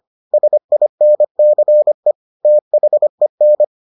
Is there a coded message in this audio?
Yes: Morse code from 0.3 to 3.6 s, heard as a keyed tone, about 25 wpm, 600 Hz. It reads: SINCE THEN